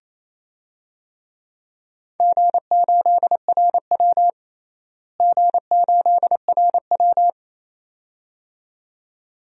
Morse code, heard as a keyed tone, about 27 wpm, 700 Hz. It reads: G8RW G8RW
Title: DXCC-100 Call Signs - 4 Characters - Repeated 2x - 28wpm